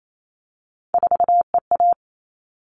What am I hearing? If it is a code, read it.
Morse code, heard as a keyed tone, about 28 wpm, 700 Hz. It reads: 4EA
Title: All Cut Numbers - 3-Digit 28wpm